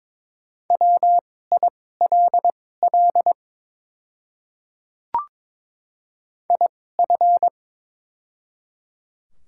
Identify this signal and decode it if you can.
Morse code, heard as a keyed tone, about 22 wpm, spaced as for 16 wpm, 700 Hz. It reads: WILL IF
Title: 2x Character Spacing - Top 200 Words - 22wpm